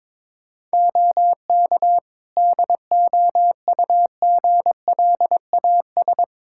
Morse code, heard as a keyed tone, about 22 wpm, 700 Hz. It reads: OK DOUGLAS
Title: Practice QSO - Exchange 3 - More Info - 22wpm